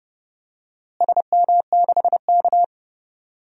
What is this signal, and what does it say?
Morse code, heard as a keyed tone, about 32 wpm, 700 Hz. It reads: SM6K
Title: DXCC-100 Call Signs - 4 Characters - Repeated 2x - 30wpm